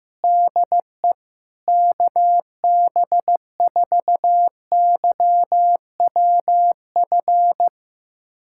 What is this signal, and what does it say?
Morse code, heard as a keyed tone, about 15 wpm, 700 Hz. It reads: DE KB4YWF
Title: USA Callsign DE Callsign - 6 Characters - 15wpm